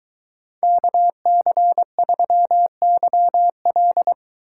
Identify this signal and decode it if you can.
Morse code, heard as a keyed tone, about 23 wpm, 700 Hz. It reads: KC3YL